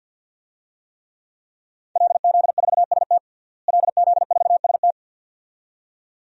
Morse code, heard as a keyed tone, about 53 wpm, 700 Hz. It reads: LZ4ST LZ4ST
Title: DXCC-100 Call Signs - 5 Characters - Repeated 2x - 50wpm